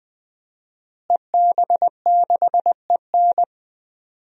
Morse code, heard as a keyed tone, about 20 wpm, 700 Hz. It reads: EB6EN
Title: DXCC-100 Call Signs - 5 Characters - Repeated 2x - 20wpm